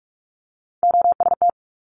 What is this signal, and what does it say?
Morse code, heard as a keyed tone, about 46 wpm, 700 Hz. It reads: OST
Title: Multi-letter Phonemes Quick Fire 45wpm